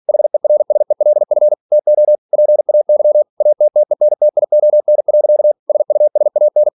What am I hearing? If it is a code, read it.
Morse code, heard as a keyed tone, about 46 wpm, 600 Hz. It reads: HERSELF TO PAY ATTENTION. SUSAN